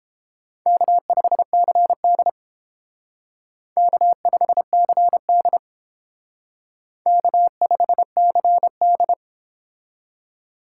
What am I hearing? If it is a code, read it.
Morse code, heard as a keyed tone, about 29 wpm, 700 Hz. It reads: K5CD K5CD K5CD